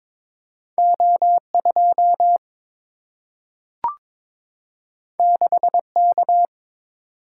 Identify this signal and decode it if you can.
Morse code, heard as a keyed tone, about 22 wpm, 700 Hz. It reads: O2 6K
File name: practice / 22wpm / Two Letter-Number Combinations - 22wpm